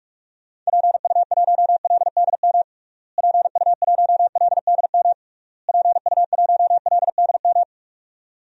Morse code, heard as a keyed tone, about 46 wpm, 700 Hz. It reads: PU1LDM PU1LDM PU1LDM
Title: DXCC-100 Call Signs - 6 Characters - Repeated 3x - 45wpm